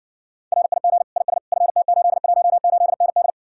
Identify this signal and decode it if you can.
Morse code, heard as a keyed tone, about 58 wpm, 700 Hz. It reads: RIG IS FT818ND